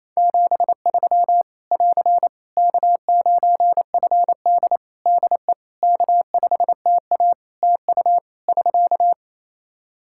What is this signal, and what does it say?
Morse code, heard as a keyed tone, about 28 wpm, 700 Hz. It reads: 73 <AR> K9FD DE K5TA TU <SK>
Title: K1USN SST QSO Practice 28wpm